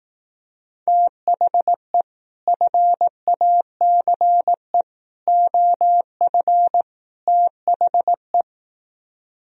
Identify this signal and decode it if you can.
Morse code, heard as a keyed tone, about 18 wpm, 700 Hz. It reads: THE FACE OF THE
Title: Speed-Racing - Sets of 4 Words 15wpm